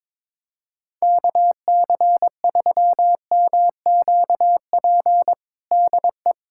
Morse code, heard as a keyed tone, about 21 wpm, 700 Hz. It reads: KC3MQP DE
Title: USA Callsign DE Callsign 22wpm